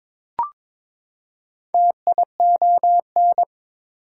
Morse code, heard as a keyed tone, about 22 wpm, 700 Hz. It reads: TION
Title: Prefixes and Suffixes 22wpm